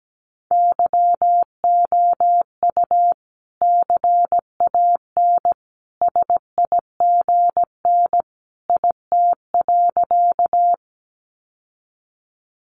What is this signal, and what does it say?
Morse code, heard as a keyed tone, about 17 wpm, 700 Hz. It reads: YOU CAN SIGN IT.
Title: Sentences from Top 1000 Words 17wpm